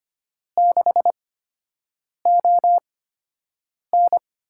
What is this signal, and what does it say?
Morse code, heard as a keyed tone, about 25 wpm, spaced as for 7 wpm, 700 Hz. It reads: 6ON
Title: USA Call Signs - 5 Characters - 8x Character Spacing - 25wpm